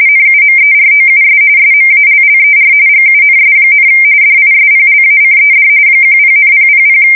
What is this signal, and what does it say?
RTTY, about 45.45 baud. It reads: NUMEROUS SMALLER GUNS, DISPLACED 10,000
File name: fsk170 / battleshipnj